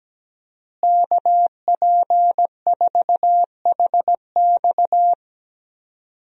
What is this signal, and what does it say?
Morse code, heard as a keyed tone, about 17 wpm, 700 Hz. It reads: KP4HX
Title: USA Call Signs - 5 Characters - Repeated 3x - 17wpm